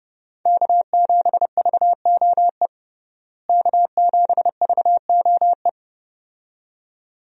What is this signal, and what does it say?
Morse code, heard as a keyed tone, about 31 wpm, 700 Hz. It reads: K7VOE K7VOE
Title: USA Call Signs - 5 Characters - Repeated 2x - 30wpm